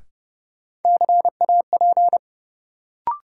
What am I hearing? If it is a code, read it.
Morse code, heard as a keyed tone, about 30 wpm, 700 Hz. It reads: CAP
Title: Sentences from Top 1600 Words - New Word Drill Alt - 30wpm